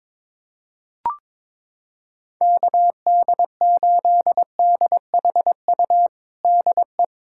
Morse code, heard as a keyed tone, about 21 wpm, 700 Hz. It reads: KD8DHU DE
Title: USA Callsign DE Callsign 22wpm